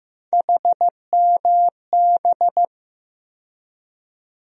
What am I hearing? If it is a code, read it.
Morse code, heard as a keyed tone, about 15 wpm, 700 Hz. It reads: HMB
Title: USA Call Sign Postfixes - 3 Letters 15wpm